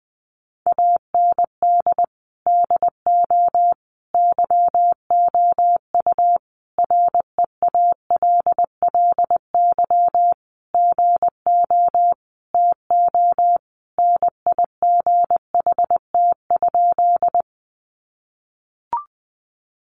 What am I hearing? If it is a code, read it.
Morse code, heard as a keyed tone, about 20 wpm, 700 Hz. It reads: AND DO YOU REALLY GO TO NIGHT?
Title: Sentences from Top 600 Words 20wpm